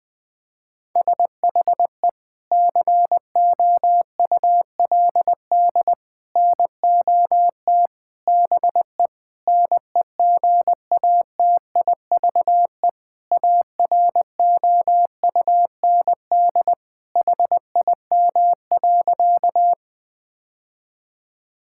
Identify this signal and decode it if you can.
Morse code, heard as a keyed tone, about 20 wpm, 700 Hz. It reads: SHE COULD NOT BE NEGATIVE AROUND HIM.
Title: Sentences from Top 1000 Words - Review Part 6 - 20wpm